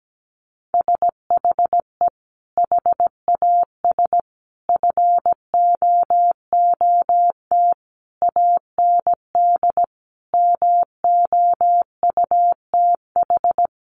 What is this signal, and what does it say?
Morse code, heard as a keyed tone, about 17 wpm, 700 Hz. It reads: SHE HAS FOOT AND MOUTH
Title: Sentences from Top 1500 Words 17wpm